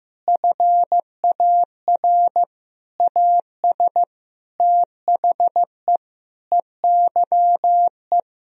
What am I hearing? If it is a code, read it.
Morse code, heard as a keyed tone, about 15 wpm, 700 Hz. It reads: FAR AS THE EYE
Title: Sentences from Top 1000 Words - Review Part 1 - 15wpm